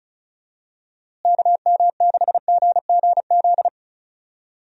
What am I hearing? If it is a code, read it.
Morse code, heard as a keyed tone, about 34 wpm, 700 Hz. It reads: KM6GGZ
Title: USA Call Signs - 6 Characters - Repeated 2x - 35wpm